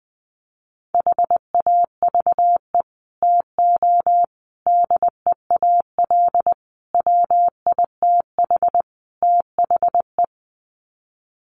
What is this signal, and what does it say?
Morse code, heard as a keyed tone, about 20 wpm, 700 Hz. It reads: HAVE TO DEAL WITH THE
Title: Sets of 5 Words Encore2 20wpm